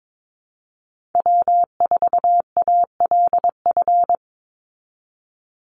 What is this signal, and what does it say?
Morse code, heard as a keyed tone, about 22 wpm, 700 Hz. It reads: W4ALF
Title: Call Signs Repeated 2x - 22wpm